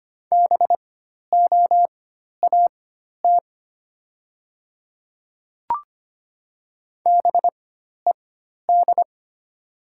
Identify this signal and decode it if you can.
Morse code, heard as a keyed tone, about 25 wpm, spaced as for 12 wpm, 700 Hz. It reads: BOAT BED
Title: 4x Character Spacing - Top 500 Words - 25wpm